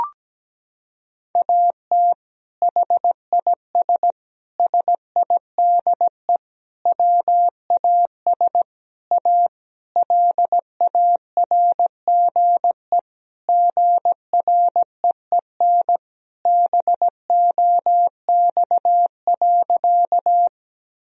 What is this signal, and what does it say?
Morse code, heard as a keyed tone, about 17 wpm, 700 Hz. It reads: AT HIS SIDE WAS A LARGE GREEN BOX.
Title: Sentences from Top 1000 Words - Review Part 2 - 17wpm